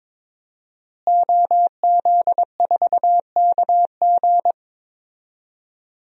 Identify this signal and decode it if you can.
Morse code, heard as a keyed tone, about 22 wpm, 700 Hz. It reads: OZ4KG
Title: DXCC-100 Call Signs - 5 Characters - Repeated 3x - 22wpm